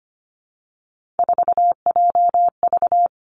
Morse code, heard as a keyed tone, about 25 wpm, 700 Hz. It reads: 4JV
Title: Three Challenging Letter-Number Combinations 25wpm